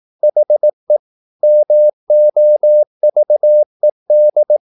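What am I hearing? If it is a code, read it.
Morse code, heard as a keyed tone, about 18 wpm, 600 Hz. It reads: HE MOVED
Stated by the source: Morse practice at 18 wpm, English text, from CW med Gnister.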